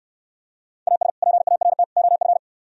Morse code, heard as a keyed tone, about 64 wpm, 700 Hz. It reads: AS PART OF